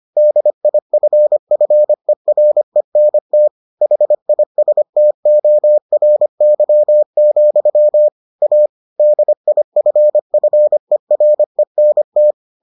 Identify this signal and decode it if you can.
Morse code, heard as a keyed tone, about 25 wpm, 600 Hz. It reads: DIFFERENT HISTORY, A DIFFERENT